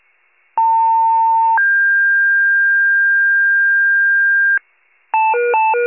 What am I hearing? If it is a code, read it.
Two-tone page, heard as a one-second tone, then a three-second tone, about 900 then 1600 Hz, with noise. It is heard from 0.6 to 4.6 s.
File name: wcalert